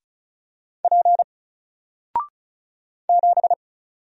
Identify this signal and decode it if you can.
Morse code, heard as a keyed tone, about 36 wpm, 700 Hz. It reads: P 7
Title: Single Letter-Number 35wpm